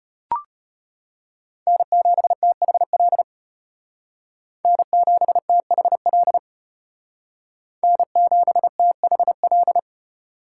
Morse code, heard as a keyed tone, about 32 wpm, 700 Hz. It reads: N7THL N7THL N7THL